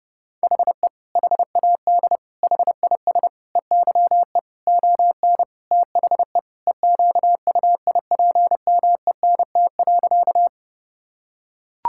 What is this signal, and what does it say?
Morse code, heard as a keyed tone, about 32 wpm, 700 Hz. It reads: HE HAD HIS EYE ON THE EQUIPMENT.
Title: Sentences from Top 1000 Words - Review Part 5 - 30wpm